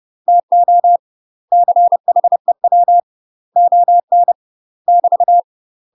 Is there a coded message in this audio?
Yes: Morse code from 0.3 to 5.4 s, heard as a keyed tone, about 30 wpm, 700 Hz. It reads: TO CHEW ON <BT>